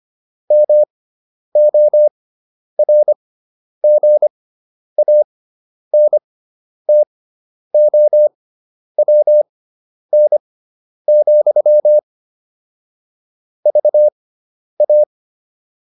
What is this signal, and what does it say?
Morse code, heard as a keyed tone, about 25 wpm, spaced as for 10 wpm, 600 Hz. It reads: MORGANTOWN, VA
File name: QSO302 10